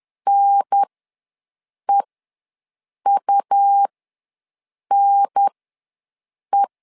Morse code, heard as a keyed tone, about 10 wpm, spaced as for 6 wpm, 800 Hz. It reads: NEUNE
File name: 2 UHDN ws 10wpm